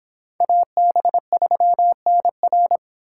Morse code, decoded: AB3NR

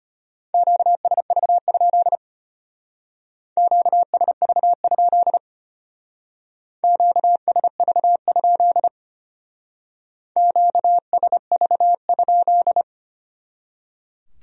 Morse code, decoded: QSV? QSV? QSV? QSV?

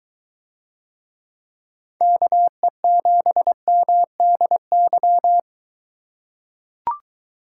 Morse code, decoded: KE7MDY

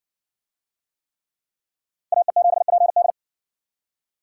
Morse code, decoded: AE7PD

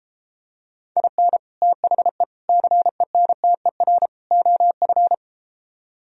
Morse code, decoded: IN THE CENTER OF

